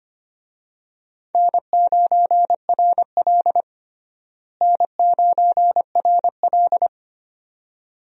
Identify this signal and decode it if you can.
Morse code, heard as a keyed tone, about 25 wpm, 700 Hz. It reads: N9RL N9RL
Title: USA Call Signs - 4 Characters - Repeated 2x - 25wpm